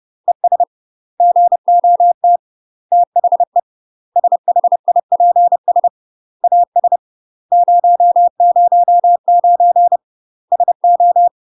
Morse code, heard as a keyed tone, about 30 wpm, 700 Hz. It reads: ES GOT THE SHIPS AS 009 SO
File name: UK MASTER QSOs TRAINING / 30WPM / UK MASTER QSOs 0018